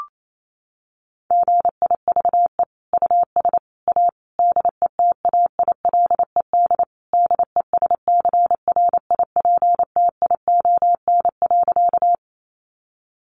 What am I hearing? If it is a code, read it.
Morse code, heard as a keyed tone, about 28 wpm, 700 Hz. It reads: GIVE US A DETAILED DESCRIPTION.
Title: Sentences from Top 1500 Words 28wpm